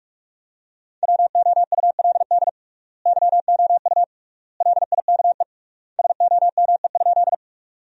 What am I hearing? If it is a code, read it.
Morse code, heard as a keyed tone, about 42 wpm, 700 Hz. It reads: WOULD YOU LIKE SOME?
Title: 3x Word-Spacing - Sentences from Top 100 Words - 45wpm